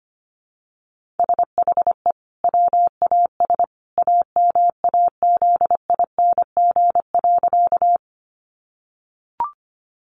Morse code, heard as a keyed tone, about 25 wpm, 700 Hz. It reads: SHE WAS AMAZING.
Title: Sentences from Top 1500 Words 25wpm